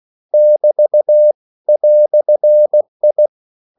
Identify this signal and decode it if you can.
Morse code, heard as a keyed tone, about 16 wpm, 600 Hz. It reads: <BT> "I
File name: CW / Practice / EnglishQuotations0216